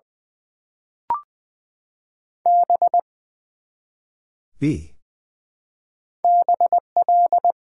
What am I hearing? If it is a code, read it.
Morse code, heard as a keyed tone, about 20 wpm, 700 Hz. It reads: B BL
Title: Build A Word - 1 Letter At A Time - 3 to 5-Letter Words - 20wpm